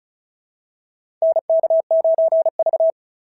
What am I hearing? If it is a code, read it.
Morse code, heard as a keyed tone, about 36 wpm, 690 Hz. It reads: NK9V
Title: SOTA QSO Practice 35wpm